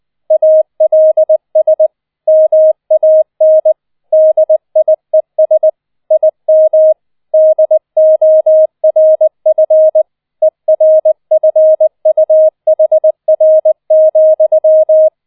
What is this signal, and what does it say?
Morse code, heard as a keyed tone, about 20 wpm, 630 Hz. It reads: ALS MAN DIES IM DORF ERFUHR,